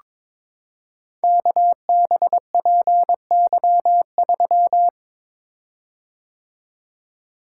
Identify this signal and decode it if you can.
Morse code, heard as a keyed tone, about 22 wpm, 700 Hz. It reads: KBPY3
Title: Code Groups - 5 Letter:Numbers - 22wpm